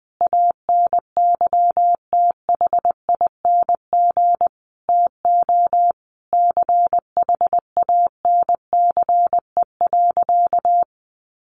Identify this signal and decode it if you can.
Morse code, heard as a keyed tone, about 20 wpm, 700 Hz. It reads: ANYTHING TO CHANCE.
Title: Sentences from Top 1100 Words 20wpm